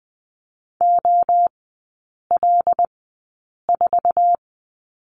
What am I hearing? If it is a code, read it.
Morse code, heard as a keyed tone, about 20 wpm, spaced as for 8 wpm, 700 Hz. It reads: OL4
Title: 074 Lesson 13e - Letter L - Three Characters 20wpm